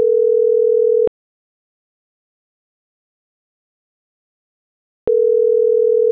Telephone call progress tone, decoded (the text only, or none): ringback tone